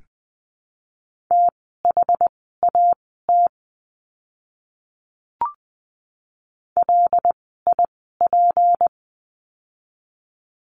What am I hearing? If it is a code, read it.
Morse code, heard as a keyed tone, about 20 wpm, spaced as for 14 wpm, 700 Hz. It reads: THAT LIP
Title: 153 Lesson 25f - Letter p - Words 20wpm